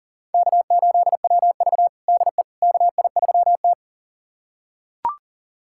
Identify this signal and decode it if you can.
Morse code, heard as a keyed tone, about 40 wpm, 700 Hz. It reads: K8WV DE KI3T
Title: USA Callsign DE Callsign - 4 Characters - 40wpm